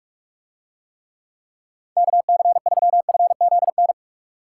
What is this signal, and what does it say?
Morse code, heard as a keyed tone, about 44 wpm, 700 Hz. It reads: KC3FZN